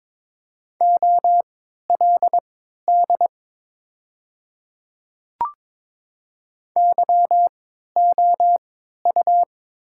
Morse code, heard as a keyed tone, about 21 wpm, spaced as for 12 wpm, 700 Hz. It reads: OLD YOU
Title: 3x Character Spacing - Top 500 Words - 22wpm